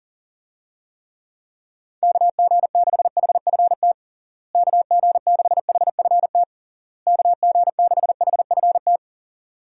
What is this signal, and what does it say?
Morse code, heard as a keyed tone, about 37 wpm, 700 Hz. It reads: KG6HFT KG6HFT KG6HFT
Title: USA Call Signs - 6 Characters - Repeated 3x - 40wpm